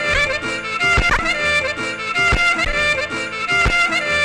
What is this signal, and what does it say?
Emergency Alert System (SAME) tones, heard as harsh buzzing bursts. It is heard from 0.7 to 4.0 s.